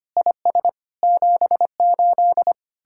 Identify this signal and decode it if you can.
Morse code, heard as a keyed tone, about 32 wpm, 700 Hz. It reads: IS 78